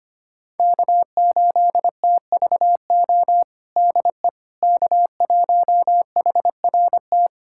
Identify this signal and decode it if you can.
Morse code, heard as a keyed tone, about 25 wpm, 700 Hz. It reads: K8TVO DE K1HRT
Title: USA Callsign DE Callsign - 5 Characters - 25wpm